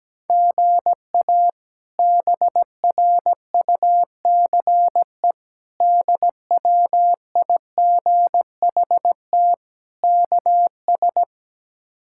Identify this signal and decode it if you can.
Morse code, heard as a keyed tone, about 17 wpm, 700 Hz. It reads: GA BRUCE DWIGHT KS
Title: K1USN SST QSO Practice - 17wpm